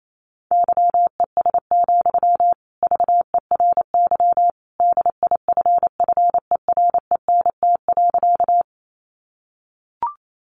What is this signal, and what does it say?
Morse code, heard as a keyed tone, about 28 wpm, 700 Hz. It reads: YES, VERY DIFFERENT.
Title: Sentences from Top 600 Words 28wpm